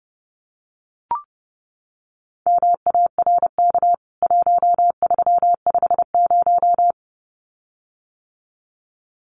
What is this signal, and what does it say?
Morse code, heard as a keyed tone, about 28 wpm, 700 Hz. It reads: MARK 1350
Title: CWT Exchange 30wpm